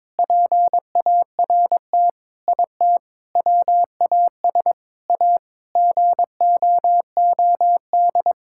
Morse code, heard as a keyed tone, about 22 wpm, 700 Hz. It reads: PART IT WAS A GOOD
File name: practice / 22wpm / Sentences with 4-Word Phrases - 22wpm